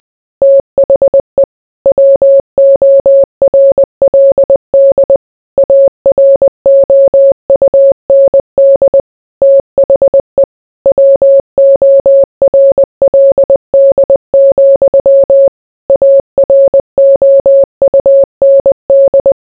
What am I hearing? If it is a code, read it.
Morse code, heard as a keyed tone, about 20 wpm, 550 Hz. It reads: THE WORLD AROUND THE WORLD, AROUND